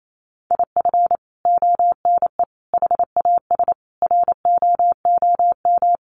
Morse code, heard as a keyed tone, about 28 wpm, 700 Hz. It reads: IF ONE HAS ROOM
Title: Sentences from Top 1400 Words 28wpm